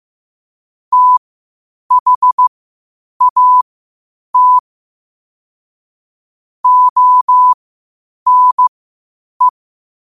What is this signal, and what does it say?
Morse code, heard as a keyed tone, about 15 wpm, spaced as for 8 wpm, 1000 Hz. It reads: THAT ONE
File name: qotdfiles / 8wpm